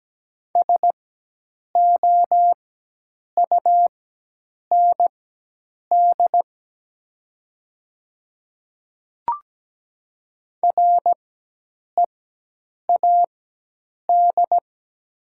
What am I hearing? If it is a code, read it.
Morse code, heard as a keyed tone, about 17 wpm, spaced as for 8 wpm, 700 Hz. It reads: SOUND READ
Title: 4x Character Spacing - Top 200 Words - 17wpm